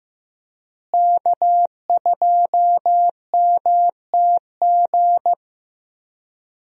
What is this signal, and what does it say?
Morse code, heard as a keyed tone, about 15 wpm, 700 Hz. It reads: K2MTG
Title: USA Call Signs - 5 Characters - Repeated 3x - 15wpm